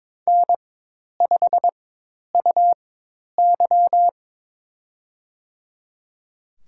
Morse code, heard as a keyed tone, about 22 wpm, spaced as for 10 wpm, 700 Hz. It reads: N5UY
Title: USA Call Signs - 4x Character Spacing - 22wpm